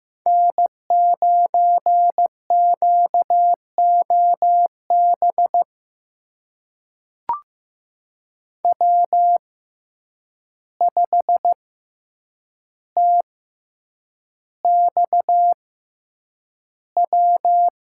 Morse code, heard as a keyed tone, about 15 wpm, spaced as for 8 wpm, 700 Hz. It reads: N9QOB W5TXW
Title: USA Call Signs - 5 Characters - 6x Character Spacing - 15wpm